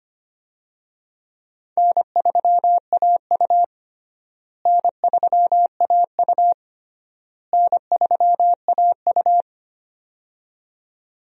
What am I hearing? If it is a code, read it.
Morse code, heard as a keyed tone, about 25 wpm, 700 Hz. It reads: N3AU N3AU N3AU
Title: USA Call Signs - 4 Characters - Repeated 3x - 25wpm